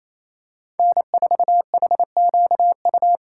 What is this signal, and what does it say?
Morse code, heard as a keyed tone, about 28 wpm, 700 Hz. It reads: N4HQU
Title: USA Call Signs - 5 Characters - 4x Character Spacing - 28wpm